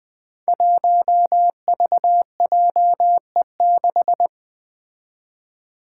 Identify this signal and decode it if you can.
Morse code, heard as a keyed tone, about 20 wpm, 700 Hz. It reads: 1VJE6
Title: Five Challenging Letter-Number Combinations 20wpm